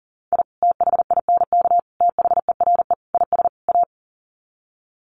Morse code, heard as a keyed tone, about 40 wpm, 700 Hz. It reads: I THINK THERE IS A
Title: Sets of 5 Words 40wpm